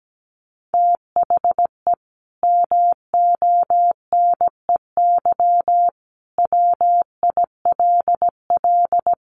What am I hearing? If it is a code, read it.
Morse code, heard as a keyed tone, about 17 wpm, 700 Hz. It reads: THE MONEY WILL
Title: Sentences from Top 1500 Words 17wpm